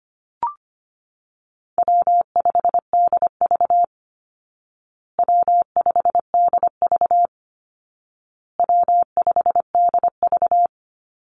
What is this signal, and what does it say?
Morse code, heard as a keyed tone, about 25 wpm, 700 Hz. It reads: W5DV W5DV W5DV
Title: Call Signs Repeated 2x - 25wpm